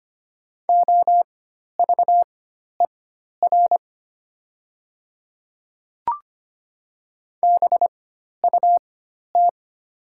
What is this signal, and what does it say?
Morse code, heard as a keyed tone, about 25 wpm, spaced as for 12 wpm, 690 Hz. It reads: OVER BUT
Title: 4x Character Spacing - Top 100 Words - 25wpm